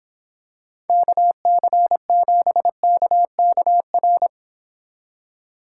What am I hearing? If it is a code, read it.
Morse code, heard as a keyed tone, about 26 wpm, 700 Hz. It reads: KC7KKR